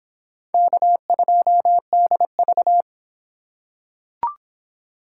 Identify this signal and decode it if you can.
Morse code, heard as a keyed tone, about 25 wpm, 700 Hz. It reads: K2DV